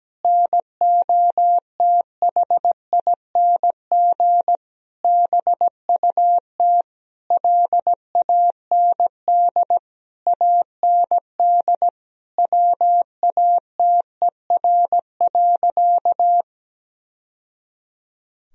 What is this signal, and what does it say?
Morse code, heard as a keyed tone, about 17 wpm, 700 Hz. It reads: NOTHING BUT LAND AND WATER.
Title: Sentences from Top 1000 Words - Review Part 1 - 17wpm